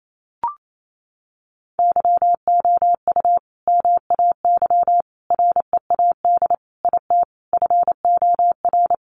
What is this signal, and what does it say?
Morse code, heard as a keyed tone, about 28 wpm, 700 Hz. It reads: YOU MAY READ IT FOR
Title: Sentences from Top 600 Words 28wpm